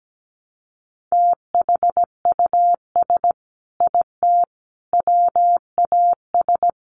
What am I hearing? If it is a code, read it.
Morse code, heard as a keyed tone, about 17 wpm, 700 Hz. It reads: THUS IT WAS
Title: Sentences from Top 800 Words 17wpm